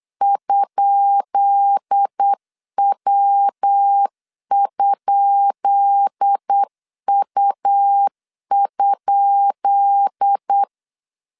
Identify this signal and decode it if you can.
Morse code, heard as a keyed tone, about 8 wpm, 800 Hz. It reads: ?W?U?
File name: LEZIONI / 10 SIGNS / 10B INT 8wpm 3534